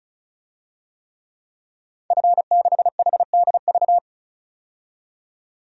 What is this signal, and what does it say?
Morse code, heard as a keyed tone, about 36 wpm, 700 Hz. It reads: F6HDV
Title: DXCC-100 Call Signs - 5 Characters - 35wpm